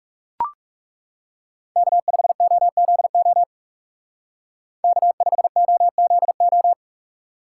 Morse code, heard as a keyed tone, about 37 wpm, 700 Hz. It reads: K5OZO K5OZO
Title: Speed Racing - USA 1x3 Call Signs 30wpm